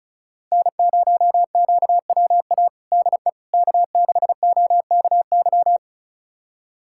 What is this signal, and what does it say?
Morse code, heard as a keyed tone, about 34 wpm, 700 Hz. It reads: N0QWA DE K6OKY
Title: USA Callsign DE Callsign - 5 Characters - 35wpm